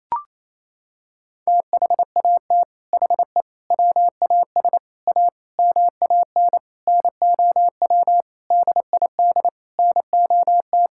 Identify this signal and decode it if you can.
Morse code, heard as a keyed tone, about 28 wpm, 700 Hz. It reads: THAT HE WAS A MAN NOW DID NOT
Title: Sentences from Top 1000 Words - Review Part 6 - 28wpm